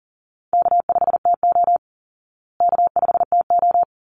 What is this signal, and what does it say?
Morse code, heard as a keyed tone, about 40 wpm, 700 Hz. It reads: K5TO K5TO
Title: Easy Call Signs Repeated 2x - 40wpm